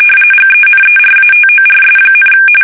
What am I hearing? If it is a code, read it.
RTTY, about 45.45 baud. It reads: GENERATORS,
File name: fsk850 / benjamin-g-lamme